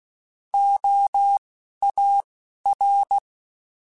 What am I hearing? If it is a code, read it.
Morse code, heard as a keyed tone, about 16 wpm, spaced as for 11 wpm, 800 Hz. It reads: OAR
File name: Alphabetical / Alphabetical mp3s / Os @10